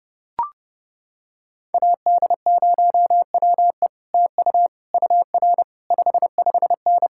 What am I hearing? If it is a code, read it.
Morse code, heard as a keyed tone, about 30 wpm, 700 Hz. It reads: AD0WE TU UR 55N